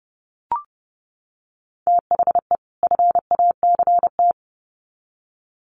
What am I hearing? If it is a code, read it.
Morse code, heard as a keyed tone, about 28 wpm, 700 Hz. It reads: THE FACT
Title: Sets of 2 Words 30wpm